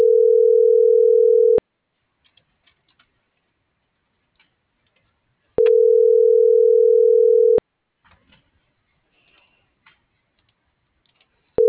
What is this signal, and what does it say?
Ringback tone (North America), 440 + 480 Hz.